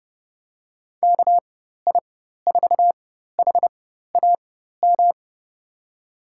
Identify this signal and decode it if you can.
Morse code, heard as a keyed tone, about 30 wpm, spaced as for 14 wpm, 700 Hz. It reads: KI4HAM
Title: USA Call Signs - 6 Characters - 4x Character Spacing - 30wpm